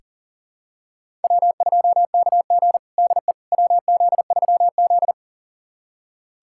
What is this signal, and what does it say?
Morse code, heard as a keyed tone, about 40 wpm, 700 Hz. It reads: W2KG DE WZ3Z
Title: USA Callsign DE Callsign - 4 Characters - 40wpm